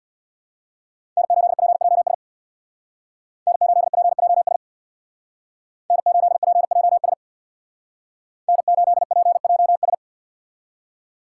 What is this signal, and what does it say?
Morse code, heard as a keyed tone, about 63 wpm, 700 Hz. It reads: N8PJS N8PJS N8PJS N8PJS